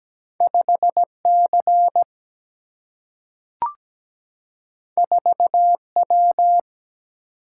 Morse code, heard as a keyed tone, about 17 wpm, 700 Hz. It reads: 5C 4W
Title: Single-Letter Single-Number 17wpm